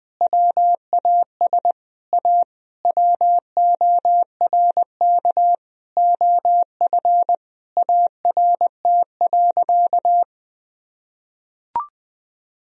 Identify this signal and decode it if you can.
Morse code, heard as a keyed tone, about 20 wpm, 700 Hz. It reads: WAS A WORK OF ART.
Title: Sentences from Top 1000 Words - Review Part 3 - 20wpm